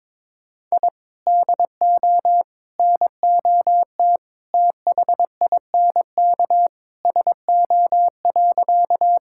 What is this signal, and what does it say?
Morse code, heard as a keyed tone, about 23 wpm, 700 Hz. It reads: I DO NOT THINK SO.
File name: practice / 22wpm / Sentences with 3-Word Phrases - 22wpm